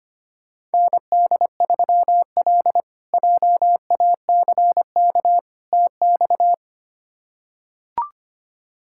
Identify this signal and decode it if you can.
Morse code, heard as a keyed tone, about 25 wpm, 700 Hz. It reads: ND3L JACK TX
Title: K1USN SST QSO Practice - 25wpm